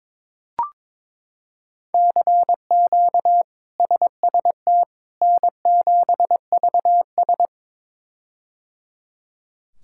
Morse code, heard as a keyed tone, about 22 wpm, 700 Hz. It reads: CQ SST N7VS
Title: K1USN SST QSO Practice - 22wpm